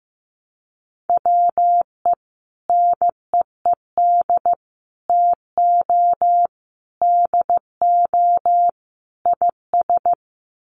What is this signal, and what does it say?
Morse code, heard as a keyed tone, about 15 wpm, 700 Hz. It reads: WE NEED TO DO IS
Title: Sets of 5 Words Encore2 15wpm